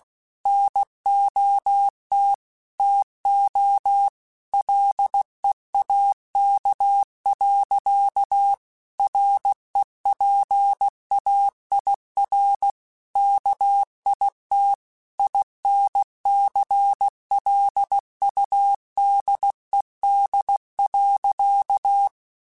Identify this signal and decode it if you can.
Morse code, heard as a keyed tone, about 16 wpm, 800 Hz. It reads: NOT TO LEAK. REPAIR KIT INCLUDED.
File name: Quotes / Quotes-mp3s / Airtight @15